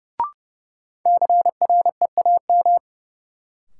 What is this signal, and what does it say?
Morse code, heard as a keyed tone, about 30 wpm, 700 Hz. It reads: CREAM
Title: Sentences from Top 1500 Words - New Word Drill - 30wpm